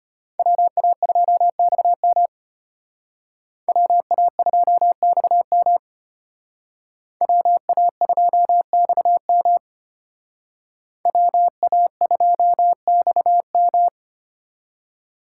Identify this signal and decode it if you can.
Morse code, heard as a keyed tone, about 33 wpm, 700 Hz. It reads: WA2XM WA2XM WA2XM WA2XM